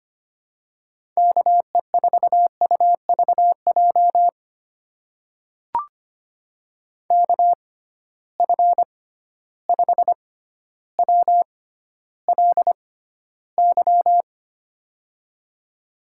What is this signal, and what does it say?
Morse code, heard as a keyed tone, about 25 wpm, spaced as for 13 wpm, 700 Hz. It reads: KE4UVJ KF5WLY
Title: USA Call Signs - 6 Characters - 6x Character Spacing - 25wpm